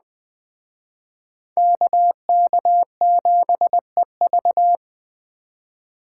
Morse code, heard as a keyed tone, about 20 wpm, 700 Hz. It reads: KK7EV
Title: Speed Racing - USA 2x2 Call Signs 17wpm